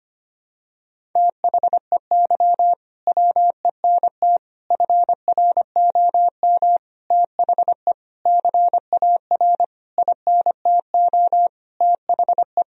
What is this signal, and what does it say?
Morse code, heard as a keyed tone, about 25 wpm, 700 Hz. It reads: THEY WENT FROM THE CAR INTO THE
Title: Sentences from Top 1000 Words - Review Part 1 - 25wpm